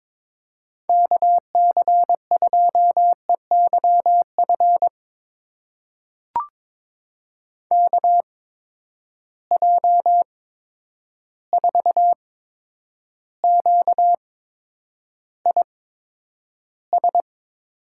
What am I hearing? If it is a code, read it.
Morse code, heard as a keyed tone, about 22 wpm, spaced as for 9 wpm, 700 Hz. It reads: KC2EYF KJ4QIS
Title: USA Call Signs - 6 Characters - 8x Character Spacing - 22wpm